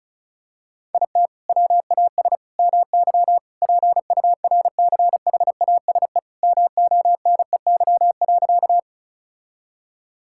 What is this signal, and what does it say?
Morse code, heard as a keyed tone, about 36 wpm, 700 Hz. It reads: IT WAS MY PURCHASE MONEY.
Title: Sentences from Top 1000 Words - Review Part 5 - 35wpm